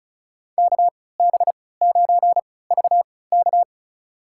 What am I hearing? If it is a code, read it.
Morse code, heard as a keyed tone, about 36 wpm, spaced as for 20 wpm, 690 Hz. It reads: KB9VK
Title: USA Call Signs - 5 Characters - 3x Character Spacing - 35wpm